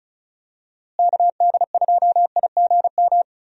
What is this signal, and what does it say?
Morse code, heard as a keyed tone, about 36 wpm, 700 Hz. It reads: KD2IGM
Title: USA Call Signs - 6 Characters - Repeated 2x - 35wpm